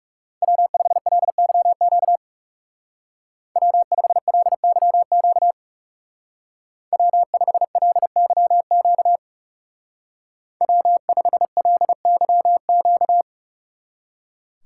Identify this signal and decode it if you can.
Morse code, heard as a keyed tone, about 37 wpm, 700 Hz. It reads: W5LYQ W5LYQ W5LYQ W5LYQ
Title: Speed Racing - USA Call Signs 30wpm